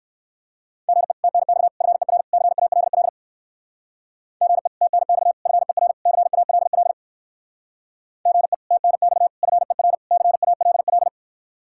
Morse code, heard as a keyed tone, about 56 wpm, 700 Hz. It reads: GE TNX FER CALL GE TNX FER CALL GE TNX FER CALL
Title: Speed-Racing Common QSO Phrases - 45wpm